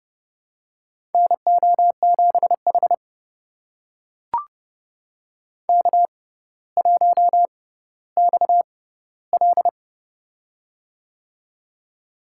Morse code, heard as a keyed tone, about 29 wpm, spaced as for 15 wpm, 700 Hz. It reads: NO7H K1XL
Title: USA Call Signs - 4 Characters - 6x Character Spacing - 30wpm